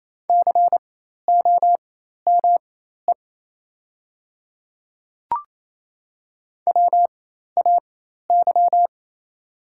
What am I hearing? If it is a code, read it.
Morse code, heard as a keyed tone, about 27 wpm, spaced as for 13 wpm, 690 Hz. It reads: COME WAY
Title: 4x Character Spacing - Top 100 Words - 28wpm